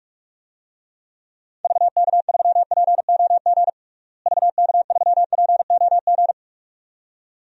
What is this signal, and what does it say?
Morse code, heard as a keyed tone, about 46 wpm, 700 Hz. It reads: VK3POG VK3POG
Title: DXCC-100 Call Signs - 6 Characters - Repeated 2x - 45wpm